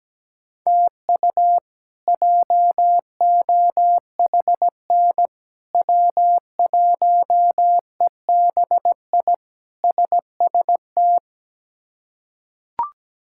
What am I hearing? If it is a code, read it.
Morse code, heard as a keyed tone, about 17 wpm, 700 Hz. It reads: TU JOHN W1EBI SST
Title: K1USN SST QSO Practice - 17wpm